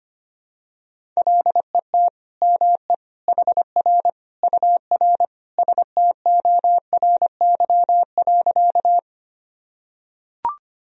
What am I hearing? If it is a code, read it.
Morse code, heard as a keyed tone, about 25 wpm, 700 Hz. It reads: LET ME HR UR STORY.
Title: Sentences from Top 300 Words and Abbreviations 25wpm